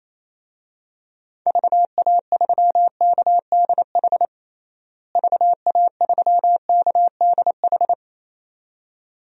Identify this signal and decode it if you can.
Morse code, heard as a keyed tone, about 28 wpm, 700 Hz. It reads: VA3KDH VA3KDH
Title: DXCC-100 Call Signs - 6 Characters - Repeated 2x - 28wpm